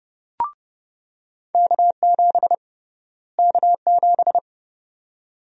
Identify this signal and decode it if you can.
Morse code, heard as a keyed tone, about 31 wpm, 700 Hz. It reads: K7 K7
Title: USA Call Sign Prefixes - Repeated 2x - 30wpm